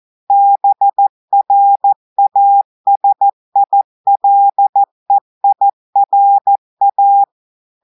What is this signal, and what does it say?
Morse code, heard as a keyed tone, about 14 wpm, 800 Hz. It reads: BRASILEIRA
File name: audios / portugais / portugais texte en clair 1 vitesse 14 mots minutes